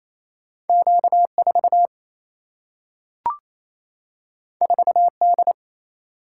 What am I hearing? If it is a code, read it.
Morse code, heard as a keyed tone, about 28 wpm, 700 Hz. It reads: Q 4 4 D
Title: Single-Letter Single-Number 28wpm